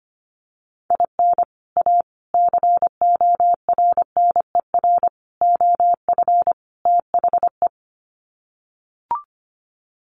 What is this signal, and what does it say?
Morse code, heard as a keyed tone, about 25 wpm, 700 Hz. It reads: IN A CORNER OF THE
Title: Sets of 5 Words Encore2 25wpm